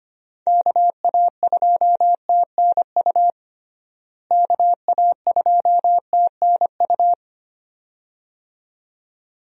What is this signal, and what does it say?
Morse code, heard as a keyed tone, about 25 wpm, 700 Hz. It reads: KA2TNU KA2TNU
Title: USA Call Signs - 6 Characters - Repeated 2x - 25wpm